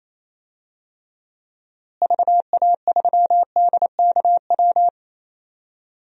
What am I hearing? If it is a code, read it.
Morse code, heard as a keyed tone, about 28 wpm, 700 Hz. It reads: VA3DKW
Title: DXCC-100 Call Signs - 6 Characters - Repeated 3x - 28wpm